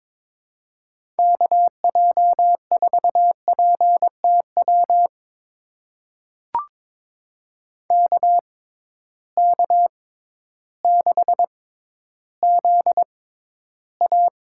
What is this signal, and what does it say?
Morse code, heard as a keyed tone, about 22 wpm, spaced as for 12 wpm, 700 Hz. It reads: KJ4PTW KK6ZA
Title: USA Call Signs - 6 Characters - 6x Character Spacing - 22wpm